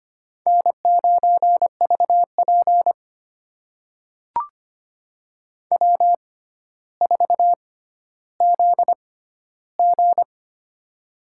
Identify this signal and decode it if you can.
Morse code, heard as a keyed tone, about 25 wpm, spaced as for 13 wpm, 700 Hz. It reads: N9VP W4ZG
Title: USA Call Signs - 4 Characters - 6x Character Spacing - 25wpm